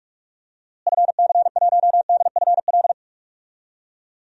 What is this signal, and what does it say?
Morse code, heard as a keyed tone, about 42 wpm, 690 Hz. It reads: FC1DFL
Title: DXCC-100 Call Signs - 6 Characters - Repeated 2x - 45wpm